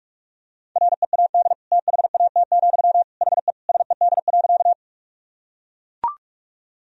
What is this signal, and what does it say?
Morse code, heard as a keyed tone, about 41 wpm, 700 Hz. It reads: READ THAT, HE SED.
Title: Sentences from Top 200 Words and Abbreviations 45wpm